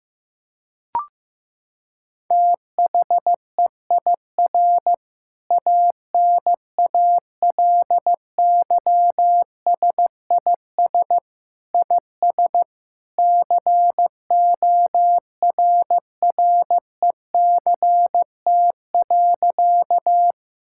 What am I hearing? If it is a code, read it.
Morse code, heard as a keyed tone, about 15 wpm, 700 Hz. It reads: THEIR ANALYSIS IS CORRECT.
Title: Sentences from Top 1000 Words - Review Part 5 - 15wpm